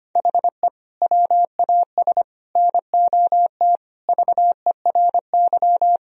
Morse code, heard as a keyed tone, about 25 wpm, 700 Hz. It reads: HE WAS NOT VERY
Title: Sentences from Top 1000 Words - Review Part 3 - 25wpm